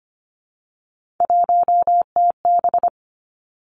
Morse code, heard as a keyed tone, about 25 wpm, 700 Hz. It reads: 1TB
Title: Three Challenging Letter-Number Combinations 25wpm